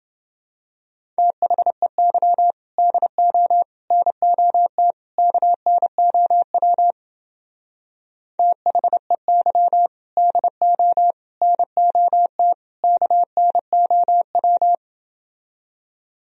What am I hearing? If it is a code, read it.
Morse code, heard as a keyed tone, about 28 wpm, 700 Hz. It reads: THEY DO NOT KNOW THEY DO NOT KNOW